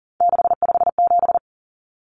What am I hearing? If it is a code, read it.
Morse code, heard as a keyed tone, about 40 wpm, 700 Hz. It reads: 657
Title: Numbers - 3-Digit 40wpm